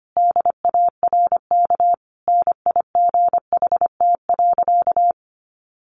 Morse code, heard as a keyed tone, about 25 wpm, 700 Hz. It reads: DARK NIGHT.
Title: Sentences from Top 1300 Words 25wpm